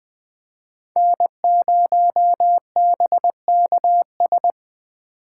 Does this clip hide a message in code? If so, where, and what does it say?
Yes: Morse code from 1.0 to 4.5 s, heard as a keyed tone, about 20 wpm, 700 Hz. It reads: N0BKS